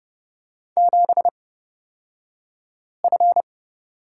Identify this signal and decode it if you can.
Morse code, heard as a keyed tone, about 30 wpm, spaced as for 5 wpm, 700 Hz. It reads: 7F
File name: Lesson 38d - Letter q - Two Characters 30wpm